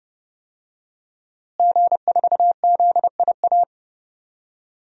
Morse code, heard as a keyed tone, about 30 wpm, 700 Hz. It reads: G4ZIA
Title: DXCC-100 Call Signs - 5 Characters - Repeated 3x - 30wpm